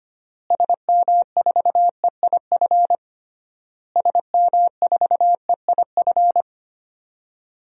Morse code, heard as a keyed tone, about 25 wpm, 700 Hz. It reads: SM4EIF SM4EIF
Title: DXCC-100 Call Signs - 6 Characters - Repeated 3x - 25wpm